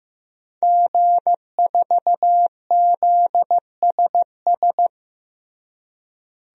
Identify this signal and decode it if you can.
Morse code, heard as a keyed tone, about 15 wpm, 700 Hz. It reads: G4ZSS
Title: DXCC-100 Call Signs - 5 Characters - Repeated 2x - 15wpm